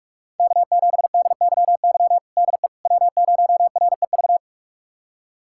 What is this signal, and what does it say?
Morse code, heard as a keyed tone, about 46 wpm, 700 Hz. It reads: K7DYY DE W0REV
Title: USA Callsign DE Callsign - 5 Characters - 45wpm